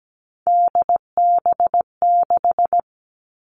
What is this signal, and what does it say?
Morse code, heard as a keyed tone, about 17 wpm, 700 Hz. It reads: DB6
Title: Three Challenging Letter-Number Combinations 17wpm